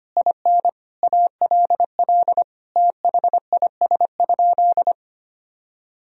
Morse code, heard as a keyed tone, about 25 wpm, 700 Hz. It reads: IN ALL THIS?
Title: Sentences from Top 1000 Words - Review Part 2 - 25wpm